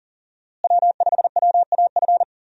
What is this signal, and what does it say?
Morse code, heard as a keyed tone, about 40 wpm, 700 Hz. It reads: W5WAF
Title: USA Call Signs - 5 Characters - Repeated 3x - 40wpm